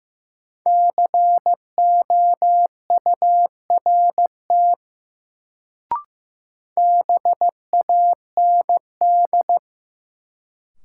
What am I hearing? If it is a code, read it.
Morse code, heard as a keyed tone, about 15 wpm, 700 Hz. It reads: COURT BAND
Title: Sentences from Top 1600 Words - New Word Drill Alt - 15wpm